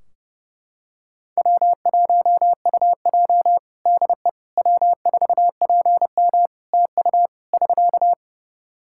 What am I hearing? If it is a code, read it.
Morse code, heard as a keyed tone, about 30 wpm, 700 Hz. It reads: W1UJ DE W4PM TU <SK>
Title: K1USN SST QSO Practice 30wpm